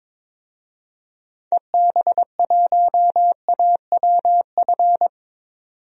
Morse code, heard as a keyed tone, about 21 wpm, 700 Hz. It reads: EB1AWF
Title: DXCC-100 Call Signs - 6 Characters - 22wpm